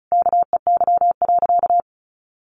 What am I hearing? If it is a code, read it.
Morse code, heard as a keyed tone, about 34 wpm, 700 Hz. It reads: KEY.
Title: Sentences from Top 1400 Words 35wpm